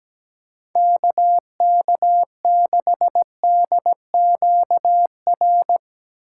Morse code, heard as a keyed tone, about 18 wpm, 700 Hz. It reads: KK6DQR
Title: Speed Racing - USA 2x3 Call Signs 17wpm